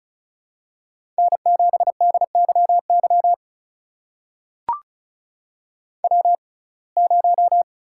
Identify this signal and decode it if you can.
Morse code, heard as a keyed tone, about 36 wpm, spaced as for 25 wpm, 700 Hz. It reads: N7DYY W0
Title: USA Call Signs - 5 Characters - 6x Character Spacing - 35wpm